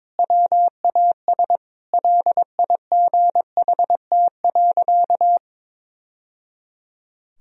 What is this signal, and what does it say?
Morse code, heard as a keyed tone, about 22 wpm, 700 Hz. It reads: WAS LIGHT.
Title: Sentences from Top 1000 Words - Review Part 1 - 22wpm